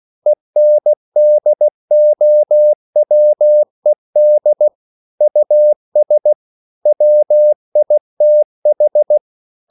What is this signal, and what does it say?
Morse code, heard as a keyed tone, about 16 wpm, 600 Hz. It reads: ENDOWED US WITH
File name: CW / Practice / EnglishQuotations0116